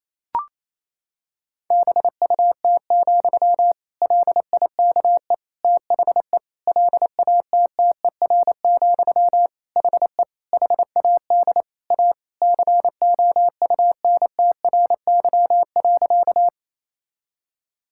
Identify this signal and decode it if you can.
Morse code, heard as a keyed tone, about 27 wpm, 700 Hz. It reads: BUT, LIKE THE LATTER, HE HAD A COUNTRY.
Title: Sentences from Top 1600 Words 28wpm